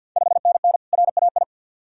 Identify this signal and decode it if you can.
Morse code, heard as a keyed tone, about 51 wpm, 690 Hz. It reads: 5NN RAI
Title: Ontario QSO Party Exchange 50wpm